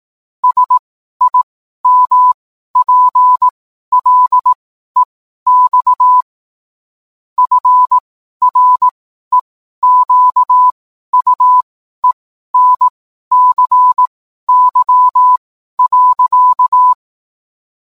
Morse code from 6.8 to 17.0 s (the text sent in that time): FREQUENCY.